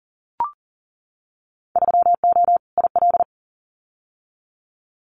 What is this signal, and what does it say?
Morse code, heard as a keyed tone, about 40 wpm, 700 Hz. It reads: 3O IL
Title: Winter Field Day Exchanges 40wpm